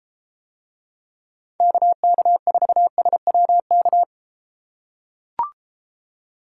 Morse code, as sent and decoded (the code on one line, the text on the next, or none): -.- -.- ....- ... .-- -.-
KK4SWK